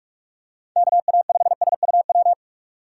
Morse code, heard as a keyed tone, about 45 wpm, 700 Hz. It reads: KA5SUW
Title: USA Call Signs - 6 Characters - 4x Character Spacing - 45wpm